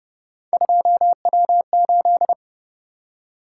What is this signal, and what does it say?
Morse code, heard as a keyed tone, about 30 wpm, 700 Hz. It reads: 2W8